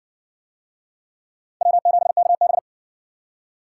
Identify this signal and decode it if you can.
Morse code, heard as a keyed tone, about 52 wpm, 700 Hz. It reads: W7KB